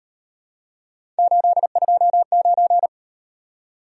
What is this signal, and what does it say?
Morse code, heard as a keyed tone, about 38 wpm, 700 Hz. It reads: 829